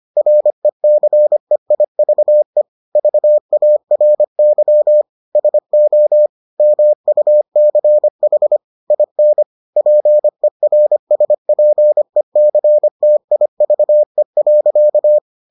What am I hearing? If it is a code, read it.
Morse code, heard as a keyed tone, about 25 wpm, 600 Hz. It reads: RECEIVE VARY SO MUCH IN PERSPECTIVE.